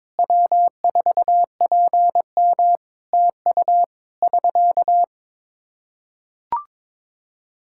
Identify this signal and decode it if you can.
Morse code, heard as a keyed tone, about 23 wpm, 700 Hz. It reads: W4PM TU <SK>
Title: Practice QSO - Exchange 4 - Ending - 22wpm